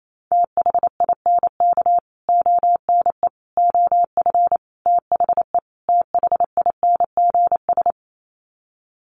Morse code, heard as a keyed tone, about 28 wpm, 700 Hz. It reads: THINK ONE OF THE THINGS
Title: Sets of 5 Words Encore2 28wpm